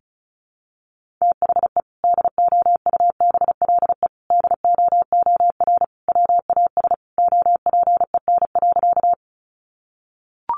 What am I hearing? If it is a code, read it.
Morse code, heard as a keyed tone, about 34 wpm, 700 Hz. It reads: THE DOUBLE DOOR WAS OPEN.
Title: Sentences from Top 1400 Words 35wpm